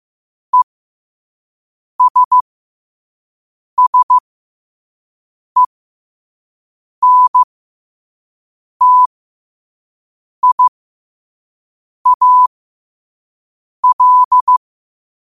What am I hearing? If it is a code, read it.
Morse code, heard as a keyed tone, about 15 wpm, spaced as for 5 wpm, 1000 Hz. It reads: ESSENTIAL
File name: qotdfiles / 5wpm